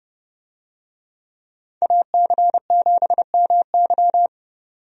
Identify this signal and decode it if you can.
Morse code, heard as a keyed tone, about 32 wpm, 700 Hz. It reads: AC7MY
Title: USA Call Signs - 5 Characters - 30wpm